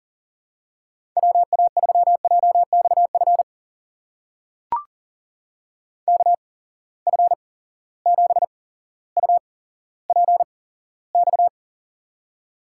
Morse code, heard as a keyed tone, about 40 wpm, spaced as for 17 wpm, 700 Hz. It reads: WA3JXF KF7UPX
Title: USA Call Signs - 6 Characters - 8x Character Spacing - 40wpm